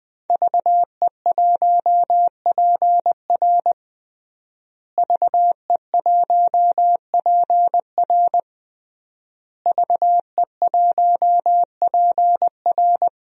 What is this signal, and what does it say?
Morse code, heard as a keyed tone, about 20 wpm, 700 Hz. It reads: VE1PR VE1PR VE1PR
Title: DXCC-100 Call Signs - 5 Characters - Repeated 3x - 20wpm